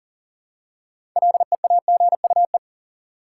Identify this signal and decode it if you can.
Morse code, heard as a keyed tone, about 40 wpm, 700 Hz. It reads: LEAGUE
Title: Six Letter Words - 40wpm